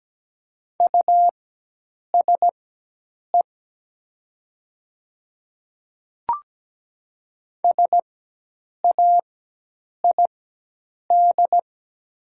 Morse code, heard as a keyed tone, about 17 wpm, spaced as for 8 wpm, 700 Hz. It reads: USE SAID
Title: 4x Character Spacing - Top 100 Words - 17wpm